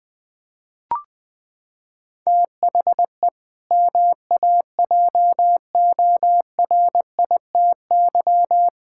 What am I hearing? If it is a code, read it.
Morse code, heard as a keyed tone, about 20 wpm, 700 Hz. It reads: THE MAJORITY
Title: Sentences with 4-Word Phrases - 20wpm